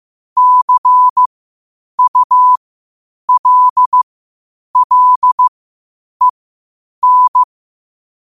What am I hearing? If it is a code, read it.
Morse code, heard as a keyed tone, about 15 wpm, spaced as for 8 wpm, 1000 Hz. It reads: CULLEN